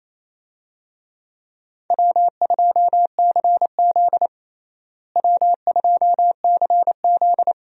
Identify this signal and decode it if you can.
Morse code, heard as a keyed tone, about 28 wpm, 700 Hz. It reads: W2CZ W2CZ
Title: USA Call Signs - 4 Characters - Repeated 2x - 28wpm